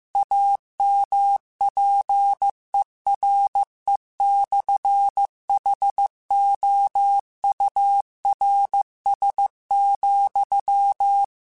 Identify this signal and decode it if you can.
Morse code, heard as a keyed tone, about 15 wpm, 800 Hz. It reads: AMPERE/HOURS,
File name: Technical Subjects / Technical Subjects-mp3s / Sw Pwr Sup @13